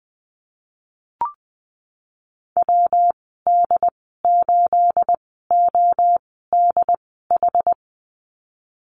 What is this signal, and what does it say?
Morse code, heard as a keyed tone, about 19 wpm, spaced as for 14 wpm, 700 Hz. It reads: WD8ODH
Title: 219 Lesson 35g - Number 0 - Call Signs 20wpm